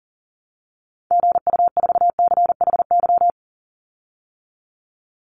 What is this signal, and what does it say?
Morse code, heard as a keyed tone, about 40 wpm, 700 Hz. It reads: GU4CHY
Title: Call Signs Repeated 3x - 40wpm